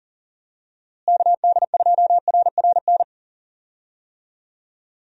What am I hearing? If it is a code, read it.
Morse code, heard as a keyed tone, about 37 wpm, 700 Hz. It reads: KD2RRN
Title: USA Call Signs - 6 Characters - Repeated 3x - 40wpm